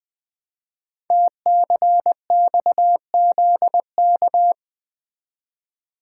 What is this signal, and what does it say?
Morse code, heard as a keyed tone, about 20 wpm, 700 Hz. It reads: TCXZK
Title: Code Groups - 5 Letters - 20wpm